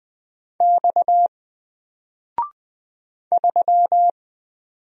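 Morse code, heard as a keyed tone, about 20 wpm, 700 Hz. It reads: X 3
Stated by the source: Single letter and number Morse drill at 20 wpm, from N4LCD.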